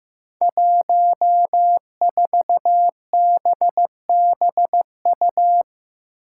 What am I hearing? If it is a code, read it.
Morse code, heard as a keyed tone, about 15 wpm, 700 Hz. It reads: 14BBU
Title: Five Challenging Letter-Number Combinations 15wpm